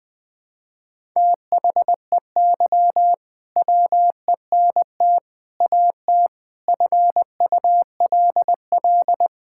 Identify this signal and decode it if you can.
Morse code, heard as a keyed tone, about 21 wpm, 700 Hz. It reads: THEY WENT AT FULL
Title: Sentences from Top 1000 Words - Review Part 5 - 20wpm